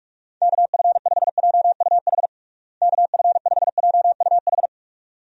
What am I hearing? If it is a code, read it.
Morse code, heard as a keyed tone, about 46 wpm, 700 Hz. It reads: KF5JUH KF5JUH
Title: USA Call Signs - 6 Characters - Repeated 2x - 45wpm